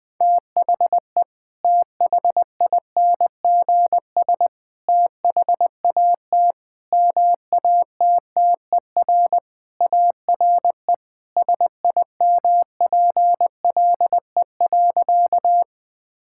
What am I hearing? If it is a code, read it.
Morse code, heard as a keyed tone, about 20 wpm, 700 Hz. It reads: THE THINGS THAT MATTER ARE SIMPLE.
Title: Sentences with 3-Word Phrases - 20wpm